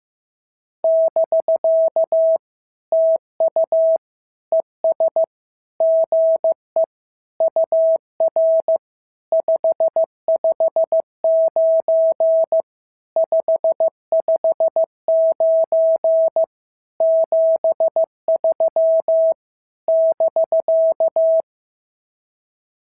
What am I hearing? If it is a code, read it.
Morse code, heard as a keyed tone, about 15 wpm, 650 Hz. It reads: <BK> TU ES GE UR 559 559 73 <BK>
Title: SOTA QSO Practice 15wpm